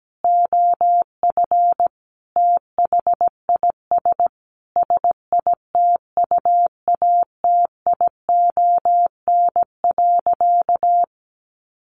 Morse code, heard as a keyed tone, about 17 wpm, 700 Hz. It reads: OF THIS SITUATION.
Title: Sentences from Top 1200 Words 17wpm